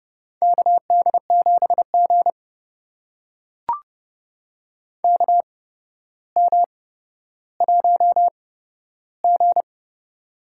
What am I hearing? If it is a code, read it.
Morse code, heard as a keyed tone, about 30 wpm, spaced as for 13 wpm, 700 Hz. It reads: KD7G KM1G
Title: USA Call Signs - 4 Characters - 8x Character Spacing - 30wpm